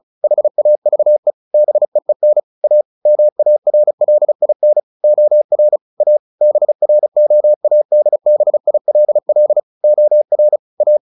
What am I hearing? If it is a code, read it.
Morse code, heard as a keyed tone, about 34 wpm, 600 Hz. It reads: HAVE BEEN A MARLIN OR A BROADBILL OR A
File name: the old man and the sea / the old man and the sea 1 35wpm